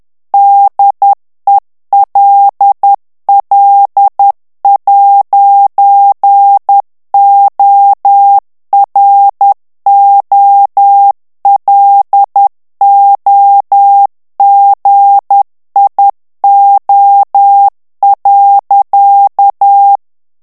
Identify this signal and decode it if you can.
Morse code, heard as a keyed tone, about 10 wpm, 800 Hz. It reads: DELL'OROLOGIO.
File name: morse mp3 / ITALIEN texte clair 1 vitesse 10 mots minute